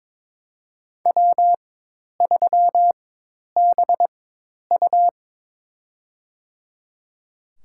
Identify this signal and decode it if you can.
Morse code, heard as a keyed tone, about 23 wpm, spaced as for 10 wpm, 700 Hz. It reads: W3BU
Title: USA Call Signs - 4 Characters - 4x Character Spacing - 22wpm